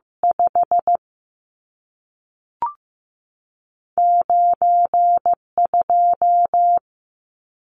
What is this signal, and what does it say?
Morse code, heard as a keyed tone, about 15 wpm, 700 Hz. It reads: 5 9 2
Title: Numbers - 2-Digit 15wpm